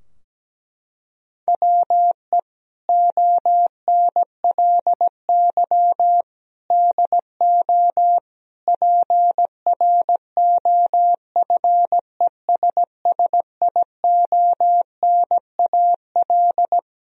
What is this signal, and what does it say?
Morse code, heard as a keyed tone, about 17 wpm, 700 Hz. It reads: WE ONLY DO PROFESSIONAL
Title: Sentences from Top 1000 Words - Review Part 4 - 17wpm